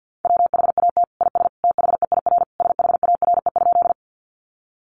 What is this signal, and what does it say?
Morse code, heard as a keyed tone, about 50 wpm, 700 Hz. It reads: WHAT IS THEIR SHARE?
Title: Sentences from Top 900 Words 50wpm